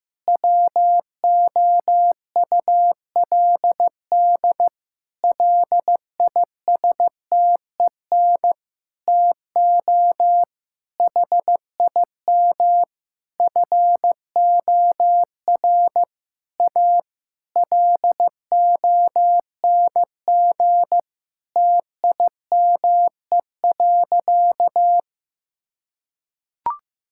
Morse code, heard as a keyed tone, about 15 wpm, 700 Hz. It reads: WOULD LISTEN TO HIM FOR A LONG TIME.
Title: Sentences from Top 1600 Words 15wpm